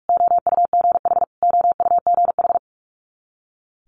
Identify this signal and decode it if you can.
Morse code, heard as a keyed tone, about 46 wpm, 700 Hz. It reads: OUGH OUGH
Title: Multi-letter Phonemes Repeated 3x 45wpm